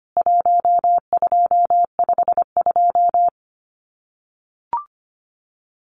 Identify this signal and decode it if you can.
Morse code, heard as a keyed tone, about 25 wpm, 700 Hz. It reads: 1252
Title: CWT Exchange 25wpm